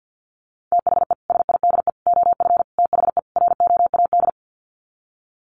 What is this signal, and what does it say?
Morse code, heard as a keyed tone, about 50 wpm, 700 Hz. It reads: THE SIDE OF THE ROAD
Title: Sets of 5 Words 50wpm